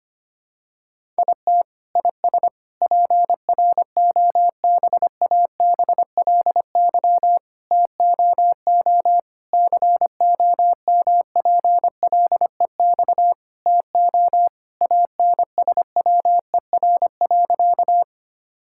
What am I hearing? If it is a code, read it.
Morse code, heard as a keyed tone, about 25 wpm, 700 Hz. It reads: IT IS PROBABLY TOO COMPLEX TO ANSWER.
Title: Sentences from Top 1000 Words - Review Part 6 - 25wpm